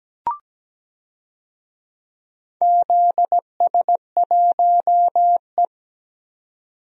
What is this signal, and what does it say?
Morse code, heard as a keyed tone, about 17 wpm, 700 Hz. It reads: ZS1E
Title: DXCC-100 Call Signs - 4 Characters - 17wpm